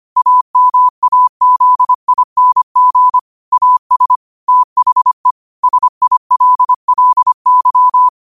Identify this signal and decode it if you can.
Morse code, heard as a keyed tone, about 25 wpm, 1000 Hz. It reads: AMAZING AS THE SILLY
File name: qotdfiles / 25wpm